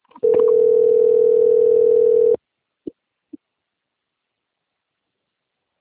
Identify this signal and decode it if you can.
Ringback tone (North America), 440 + 480 Hz.